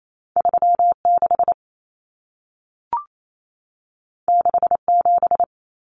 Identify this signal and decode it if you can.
Morse code, heard as a keyed tone, about 28 wpm, 700 Hz. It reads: 36 67
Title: Numbers - 2-Digit 28wpm